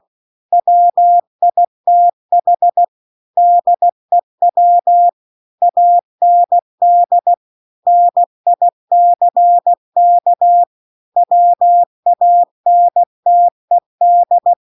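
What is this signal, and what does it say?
Morse code, heard as a keyed tone, about 16 wpm, 700 Hz. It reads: WITH DEW AND NICK WANTED
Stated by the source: Morse practice text at 16 WPM.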